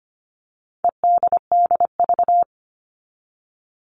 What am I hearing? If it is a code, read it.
Morse code, heard as a keyed tone, about 25 wpm, 700 Hz. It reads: EDDV
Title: All Cut Numbers - 4-Digit 25wpm